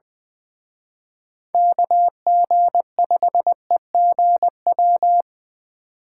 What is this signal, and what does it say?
Morse code, heard as a keyed tone, about 23 wpm, 700 Hz. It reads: KG5EGW
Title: Speed Racing - USA 2x3 Call Signs 20wpm